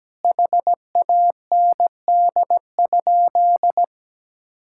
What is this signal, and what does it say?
Morse code, heard as a keyed tone, about 17 wpm, 700 Hz. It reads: HAND?
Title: Sentences from Top 1000 Words - Review Part 5 - 17wpm